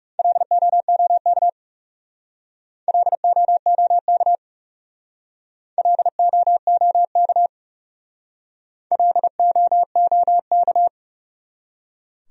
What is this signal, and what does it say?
Morse code, heard as a keyed tone, about 39 wpm, 700 Hz. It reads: LOOK LOOK LOOK LOOK